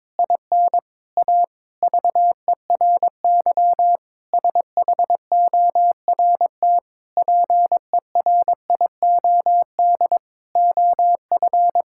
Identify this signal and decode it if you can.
Morse code, heard as a keyed tone, about 22 wpm, 700 Hz. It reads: IN A VERY SHORT PERIOD OF
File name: Sentences with 7-Word Phrases - 22wpm